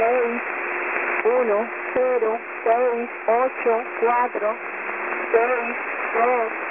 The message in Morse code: none